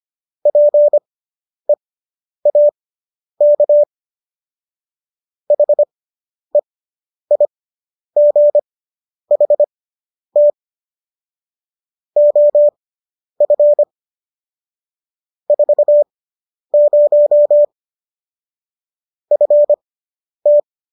Morse code, heard as a keyed tone, about 25 wpm, spaced as for 10 wpm, 590 Hz. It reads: PEAK HEIGHT OF 40 FT